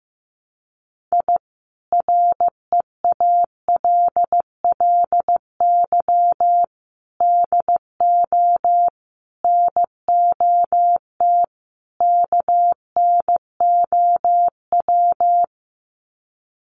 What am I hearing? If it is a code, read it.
Morse code, heard as a keyed tone, about 15 wpm, 700 Hz. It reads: I REALLY DO NOT KNOW
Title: Sets of 5 Words Encore 15wpm